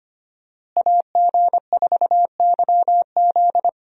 Morse code, heard as a keyed tone, about 25 wpm, 700 Hz. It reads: AG4YZ
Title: USA Call Signs - 5 Characters - Repeated 2x - 25wpm